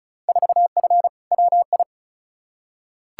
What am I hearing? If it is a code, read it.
Morse code, heard as a keyed tone, about 34 wpm, 700 Hz. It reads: 4F WI
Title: Field Day Exchanges - 35wpm